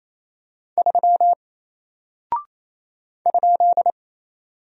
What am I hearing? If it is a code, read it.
Morse code, heard as a keyed tone, about 27 wpm, 700 Hz. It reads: 3 ?
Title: Single Character (Letter:Number:Punctuation:Prosign) Alt 28wpm